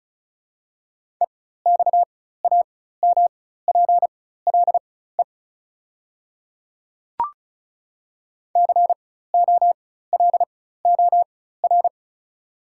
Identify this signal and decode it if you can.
Morse code, heard as a keyed tone, about 34 wpm, spaced as for 16 wpm, 700 Hz. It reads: EXAMPLE COLOR
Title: 4x Character Spacing - Top 300 Words - 35wpm